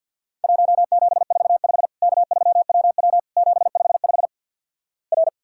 Morse code, heard as a keyed tone, about 52 wpm, 700 Hz. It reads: 1745 K3WW 755 R
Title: Worked All Europe DX Contest - QTC Exchange - 50wpm